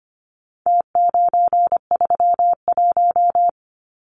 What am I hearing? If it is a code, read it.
Morse code, heard as a keyed tone, about 25 wpm, 700 Hz. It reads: T931
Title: Numbers - Contest Style - 4-Digit 25wpm